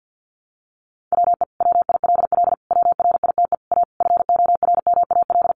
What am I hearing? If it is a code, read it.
Morse code, heard as a keyed tone, about 50 wpm, 700 Hz. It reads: WE WILL WRITE A FORMAL
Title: Sentences from Top 1300 Words 50wpm